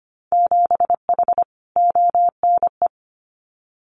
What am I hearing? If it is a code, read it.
Morse code, heard as a keyed tone, about 25 wpm, 700 Hz. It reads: 7H ONE
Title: Winter Field Day Exchanges Encore 25wpm